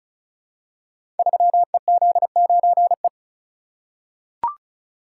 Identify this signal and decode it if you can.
Morse code, heard as a keyed tone, about 36 wpm, 700 Hz. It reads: 3EZ9E
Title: Code Groups - 5 Letter:Numbers - 35wpm